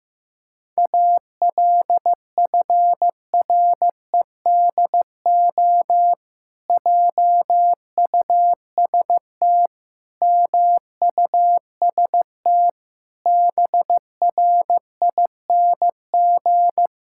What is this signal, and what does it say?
Morse code, heard as a keyed tone, about 15 wpm, 700 Hz. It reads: ALFREDO JUST MUST BRING
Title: Pangram Challenge v2 - 15wpm